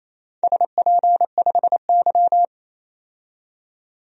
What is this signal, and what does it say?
Morse code, heard as a keyed tone, about 28 wpm, 700 Hz. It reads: SP5Y
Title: DXCC-100 Call Signs - 4 Characters - 28wpm